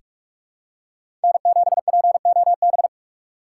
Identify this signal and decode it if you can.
Morse code, heard as a keyed tone, about 46 wpm, 700 Hz. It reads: N7POB
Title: USA Call Signs - 5 Characters - 3x Character Spacing - 45wpm